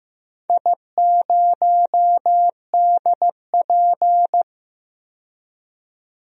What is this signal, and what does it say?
Morse code, heard as a keyed tone, about 15 wpm, 700 Hz. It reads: I0DP
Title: DXCC-100 Call Signs - 4 Characters - 15wpm